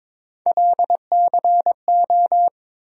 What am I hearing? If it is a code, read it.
Morse code, heard as a keyed tone, about 22 wpm, 700 Hz. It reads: LCO
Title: USA Call Sign Postfixes - 3 Letters 22wpm